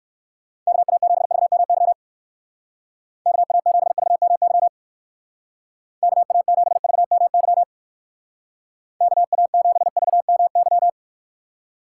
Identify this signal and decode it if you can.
Morse code, heard as a keyed tone, about 56 wpm, 700 Hz. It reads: KA7VMY KA7VMY KA7VMY KA7VMY
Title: Speed Racing - USA Call Signs 45wpm